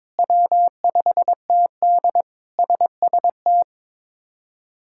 Morse code, heard as a keyed tone, about 22 wpm, 700 Hz. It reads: W5TD SST
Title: K1USN SST QSO Practice - 22wpm